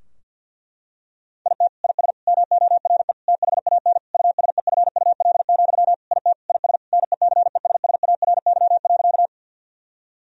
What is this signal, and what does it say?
Morse code, heard as a keyed tone, about 50 wpm, 700 Hz. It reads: IT IS MORE THAN USEFUL, IT IS NECESSARY.
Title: Sentences from Top 1000 Words - Review Part 4 - 50wpm